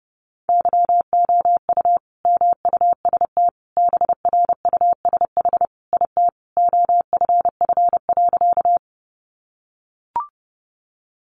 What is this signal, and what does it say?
Morse code, heard as a keyed tone, about 30 wpm, 700 Hz. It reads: YOU MUST BRUSH IT OFF.
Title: Sentences from Top 1100 Words 30wpm